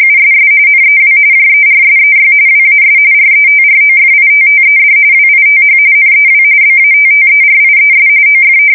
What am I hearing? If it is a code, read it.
RTTY, about 45.45 baud. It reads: USS NEW JERSEY WAS RETROFITTED WITH NEW BATTERIES OF
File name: fsk170 / battleshipnj